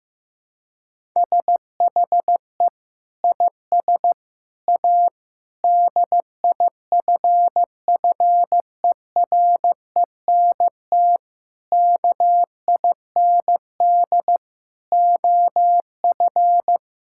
Morse code, heard as a keyed tone, about 15 wpm, 700 Hz. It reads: SHE IS A DIFFERENT KIND OF
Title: Sentences from Top 1000 Words - Review Part 5 - 15wpm